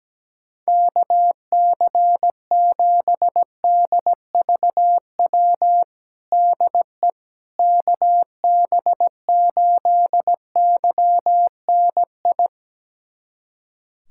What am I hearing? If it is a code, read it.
Morse code, heard as a keyed tone, about 17 wpm, 700 Hz. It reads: KC7DVW DE KB8YNI
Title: USA Callsign DE Callsign 17wpm